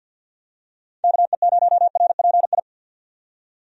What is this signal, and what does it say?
Morse code, heard as a keyed tone, about 50 wpm, 700 Hz. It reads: KE0RPI
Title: USA Call Signs - 6 Characters - 50wpm